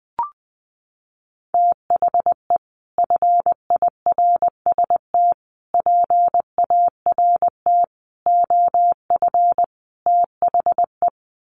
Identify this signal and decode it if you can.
Morse code, heard as a keyed tone, about 20 wpm, 700 Hz. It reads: THE FIRST PART OF THE
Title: Sets of 5 Words Encore 20wpm